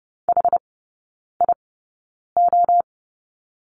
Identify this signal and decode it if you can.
Morse code, heard as a keyed tone, about 30 wpm, spaced as for 9 wpm, 700 Hz. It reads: HIO
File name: 081 Lesson 14e - Number 2 - Three Characters 30wpm